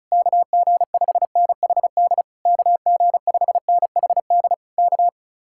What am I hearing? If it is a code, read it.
Morse code, heard as a keyed tone, about 40 wpm, 700 Hz. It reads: KG5NHD KG5NHD K